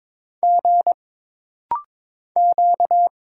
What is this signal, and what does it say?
Morse code, heard as a keyed tone, about 23 wpm, 700 Hz. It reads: G Q
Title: Single Letters - Rapid-Fire 22wpm